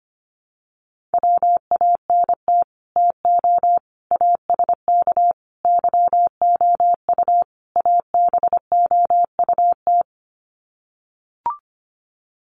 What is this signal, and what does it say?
Morse code, heard as a keyed tone, about 25 wpm, 700 Hz. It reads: WANT TO ASK YOU ABOUT
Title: Sets of 5 Words Encore2 25wpm